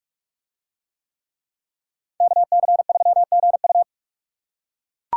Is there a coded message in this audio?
Yes: Morse code from 2.2 to 3.8 s, heard as a keyed tone, about 42 wpm, 700 Hz. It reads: KC3GU